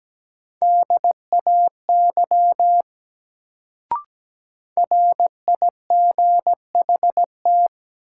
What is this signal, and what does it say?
Morse code, heard as a keyed tone, about 17 wpm, 700 Hz. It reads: DAY RIGHT
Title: Sets of 2 Words - New Word Drill - 17wpm